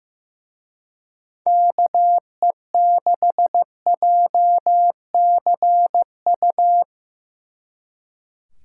Morse code, heard as a keyed tone, about 17 wpm, 700 Hz. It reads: KE6JCU
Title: Speed Racing - USA 2x3 Call Signs 15wpm